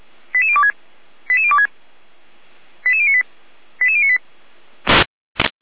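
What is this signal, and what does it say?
Five-tone selective call (ZVEI), heard as quick runs of five tones, tones of about 70 ms. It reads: 88026 88026 88098 88098
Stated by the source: ZVEI five-tone selective call, SDR reception.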